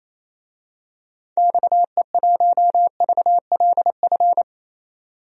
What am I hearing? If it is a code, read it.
Morse code, heard as a keyed tone, about 28 wpm, 700 Hz. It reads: XE1VLF
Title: DXCC-100 Call Signs - 6 Characters - 28wpm